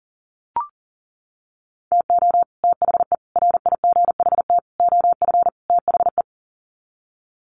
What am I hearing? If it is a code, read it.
Morse code, heard as a keyed tone, about 40 wpm, 700 Hz. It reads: TO THE RIGHT OF THE
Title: Sets of 5 Words Encore 40wpm